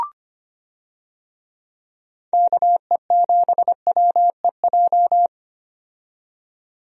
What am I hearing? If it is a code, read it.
Morse code, heard as a keyed tone, about 25 wpm, 700 Hz. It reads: KE7WEJ
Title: USA Call Signs - 6 Characters - 25wpm